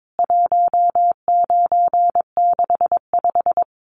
Morse code, heard as a keyed tone, about 22 wpm, 700 Hz. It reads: 1965
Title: CWT Exchange 22wpm